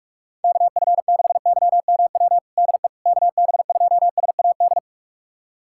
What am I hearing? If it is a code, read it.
Morse code, heard as a keyed tone, about 46 wpm, 700 Hz. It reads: KF6YMW DE KB2SAD
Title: USA Callsign DE Callsign 45wpm